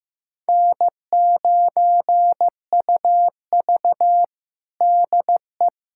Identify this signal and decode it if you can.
Morse code, heard as a keyed tone, about 15 wpm, 700 Hz. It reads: N9UV DE
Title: USA Callsign DE Callsign - 4 Characters - 15wpm